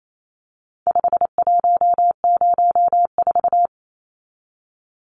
Morse code, heard as a keyed tone, about 28 wpm, 700 Hz. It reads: 5104
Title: Numbers - 4-Digit 28wpm